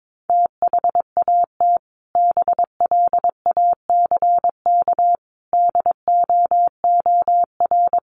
Morse code, heard as a keyed tone, about 22 wpm, 700 Hz. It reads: THAT BLACK DOOR
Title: Sentences from Top 1400 Words 22wpm